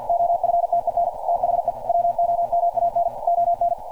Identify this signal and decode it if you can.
Morse code, heard as a keyed tone, about 51 wpm, 690 Hz. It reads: OFTEN THE WOMAN ON
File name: archived / CRSnet